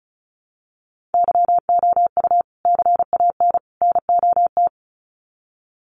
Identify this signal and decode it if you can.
Morse code, heard as a keyed tone, about 34 wpm, 700 Hz. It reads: YOU CAN NOT
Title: Sets of 3 Words 35wpm